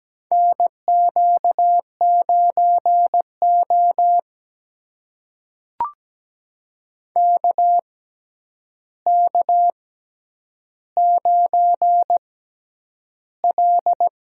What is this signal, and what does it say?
Morse code, heard as a keyed tone, about 17 wpm, spaced as for 9 wpm, 700 Hz. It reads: NQ9O KK9L
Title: USA Call Signs - 4 Characters - 6x Character Spacing - 17wpm